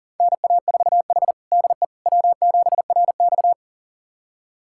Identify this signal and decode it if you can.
Morse code, heard as a keyed tone, about 40 wpm, 700 Hz. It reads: NA4H DE W7RX
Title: USA Callsign DE Callsign - 4 Characters - 40wpm